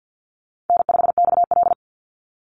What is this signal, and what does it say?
Morse code, heard as a keyed tone, about 48 wpm, 700 Hz. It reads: N5XL
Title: NA SOTA Chaser Callsigns 50wpm